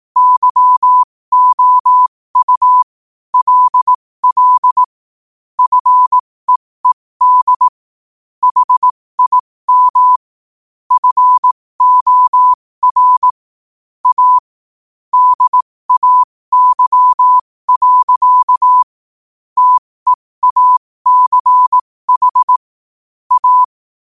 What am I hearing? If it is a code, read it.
Morse code, heard as a keyed tone, about 18 wpm, spaced as for 15 wpm, 1000 Hz. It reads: YOU LL FEED HIM FOR A DAY. TEACH A
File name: qotdfiles / 15wpm